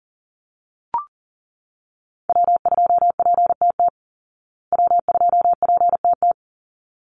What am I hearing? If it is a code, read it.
Morse code, heard as a keyed tone, about 40 wpm, 700 Hz. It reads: W2PTT W2PTT
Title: Call Signs Repeated 3x - 40wpm